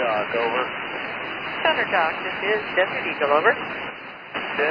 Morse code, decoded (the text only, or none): none